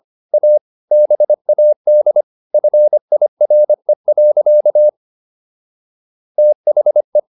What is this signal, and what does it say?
Morse code, heard as a keyed tone, about 25 wpm, 600 Hz. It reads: A BAD FIRE. THE